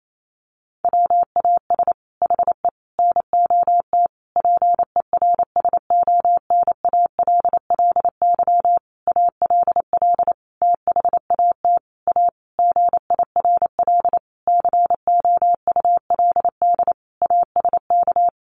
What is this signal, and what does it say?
Morse code, heard as a keyed tone, about 28 wpm, 700 Hz. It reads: WAS HE NOT PERSONALLY ALL THAT A GIRL COULD ASK
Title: Sentences from Top 1400 Words 28wpm